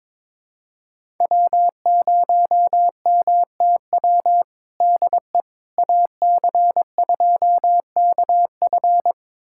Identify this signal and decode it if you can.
Morse code, heard as a keyed tone, about 22 wpm, 700 Hz. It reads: W0MTW DE AC2KF
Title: USA Callsign DE Callsign - 5 Characters - 22wpm